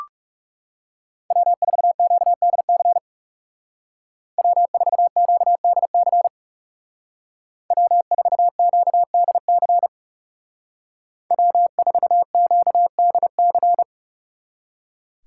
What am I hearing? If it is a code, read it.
Morse code, heard as a keyed tone, about 39 wpm, 700 Hz. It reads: W4QDC W4QDC W4QDC W4QDC